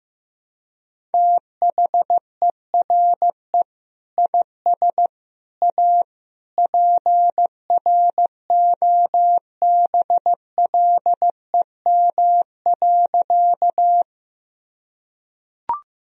Morse code, heard as a keyed tone, about 16 wpm, 700 Hz. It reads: THERE IS A PROBLEM.
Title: Sentences from Top 1000 Words - Review Part 2 - 15wpm